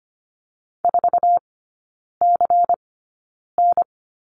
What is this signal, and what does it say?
Morse code, heard as a keyed tone, about 25 wpm, spaced as for 9 wpm, 700 Hz. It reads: 4CN
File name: course / 25:10wpm / 093 Lesson 16e - Letter c - Three Characters 25wpm